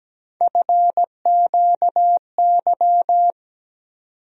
Morse code, heard as a keyed tone, about 17 wpm, 700 Hz. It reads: FQY
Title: USA Call Sign Postfixes - 3 Letters 17wpm